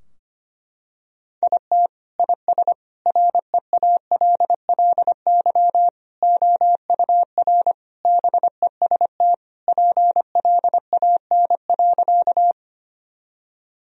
Morse code, heard as a keyed tone, about 25 wpm, 700 Hz. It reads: IT IS REALLY OUR BEST PLAN.
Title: Sentences from Top 1000 Words - Review Part 2 - 25wpm